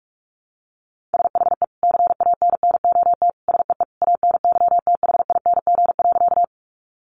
Morse code, heard as a keyed tone, about 44 wpm, 700 Hz. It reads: SHE CANNOT SEE ANYTHING.
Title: Additional Easy Phrases 45wpm